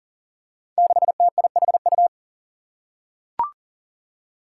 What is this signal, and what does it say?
Morse code, heard as a keyed tone, about 40 wpm, 700 Hz. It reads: 6TIHU
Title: Five Challenging Letter-Number Combinations 40wpm